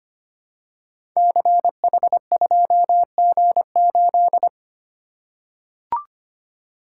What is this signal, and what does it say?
Morse code, heard as a keyed tone, about 25 wpm, 690 Hz. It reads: CH2G8
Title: Code Groups - 5 Letter:Numbers - 25wpm